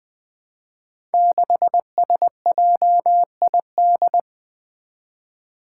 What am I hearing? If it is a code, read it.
Morse code, heard as a keyed tone, about 20 wpm, 700 Hz. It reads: 6SJID
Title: Five Challenging Letter-Number Combinations 20wpm